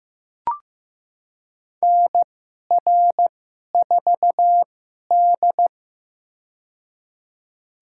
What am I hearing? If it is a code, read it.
Morse code, heard as a keyed tone, about 15 wpm, spaced as for 11 wpm, 700 Hz. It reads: NR4D
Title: USA Call Signs - 4 Characters - 2x Character Spacing - 15wpm